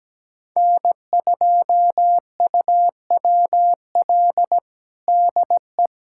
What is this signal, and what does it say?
Morse code, heard as a keyed tone, about 17 wpm, 700 Hz. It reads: N2UWL DE
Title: USA Callsign DE Callsign - 5 Characters - 17wpm